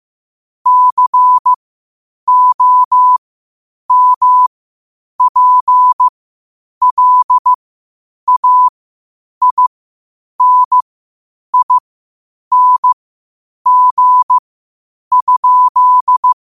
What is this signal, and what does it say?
Morse code, heard as a keyed tone, about 15 wpm, spaced as for 8 wpm, 1000 Hz. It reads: COMPLAINING?
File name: qotdfiles / 8wpm